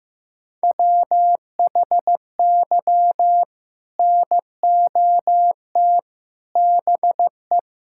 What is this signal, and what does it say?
Morse code, heard as a keyed tone, about 16 wpm, 700 Hz. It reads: WHY NOT BE
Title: Sentences from Top 1600 Words 15wpm